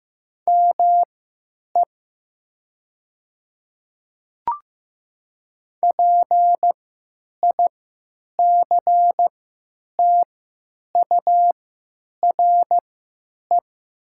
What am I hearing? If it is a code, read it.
Morse code, heard as a keyed tone, about 15 wpm, spaced as for 9 wpm, 700 Hz. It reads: ME PICTURE
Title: 3x Character Spacing - Top 200 Words - 15wpm